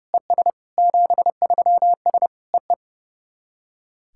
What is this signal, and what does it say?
Morse code, heard as a keyed tone, about 30 wpm, 700 Hz. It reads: ES 73S EE
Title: SOTA QSO Practice 30wpm